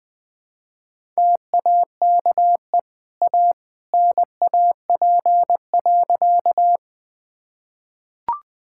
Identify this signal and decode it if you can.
Morse code, heard as a keyed tone, about 21 wpm, 700 Hz. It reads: TAKE A NAP.
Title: Easy Sentences with Hints - 20wpm